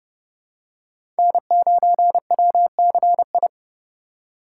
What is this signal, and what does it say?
Morse code, heard as a keyed tone, about 30 wpm, 700 Hz. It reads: N9WCI
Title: USA Call Signs - 2x Character Spacing - 30wpm